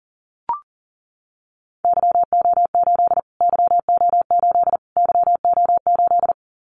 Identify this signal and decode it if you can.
Morse code, heard as a keyed tone, about 40 wpm, 700 Hz. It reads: YO8 YO8 YO8
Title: Callsign Prefixes for Top 25 DXCC - Repeated 3x - 40wpm